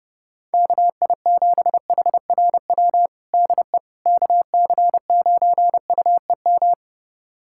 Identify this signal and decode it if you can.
Morse code, heard as a keyed tone, about 30 wpm, 700 Hz. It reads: KI7HRW DE KC9UEM
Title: USA Callsign DE Callsign - 6 Characters - 30wpm